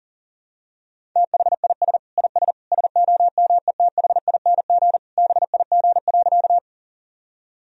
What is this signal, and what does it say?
Morse code, heard as a keyed tone, about 40 wpm, 700 Hz. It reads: THIS IS SOMETHING BIG.
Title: Sentences from Top 1000 Words - Review Part 1 - 40wpm